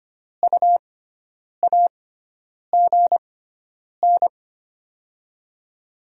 Morse code, heard as a keyed tone, about 25 wpm, spaced as for 9 wpm, 690 Hz. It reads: U AGN
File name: Top 20 QSO Elements - 6x Character Spacing - 25wpm